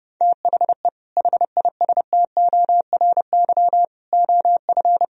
Morse code, heard as a keyed tone, about 29 wpm, 700 Hz. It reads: THE HISTORY OF